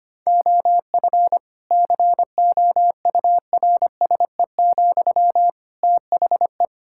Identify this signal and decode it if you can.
Morse code, heard as a keyed tone, about 25 wpm, 700 Hz. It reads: OF COURSE, THE
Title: Sentences from Top 1600 Words 25wpm